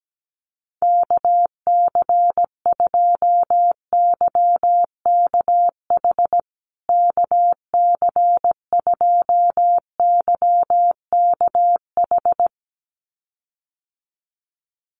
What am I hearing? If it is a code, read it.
Morse code, heard as a keyed tone, about 17 wpm, 700 Hz. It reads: KC2YKH KC2YKH
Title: USA Call Signs Repeated 2x 17wpm